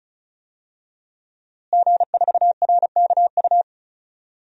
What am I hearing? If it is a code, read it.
Morse code, heard as a keyed tone, about 34 wpm, 700 Hz. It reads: G4RKU
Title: DXCC-100 Call Signs - 5 Characters - Repeated 2x - 35wpm